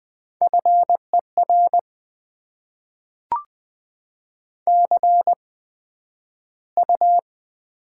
Morse code, heard as a keyed tone, about 20 wpm, 700 Hz. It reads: F E R C U
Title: Top 20 QSO Elements - 8x Character Spacing - 20wpm